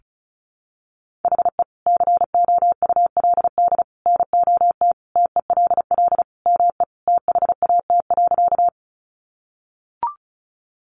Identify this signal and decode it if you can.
Morse code, heard as a keyed tone, about 34 wpm, 700 Hz. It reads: HE COULD NOT TELL ME THAT.
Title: Sentences from Top 600 Words 35wpm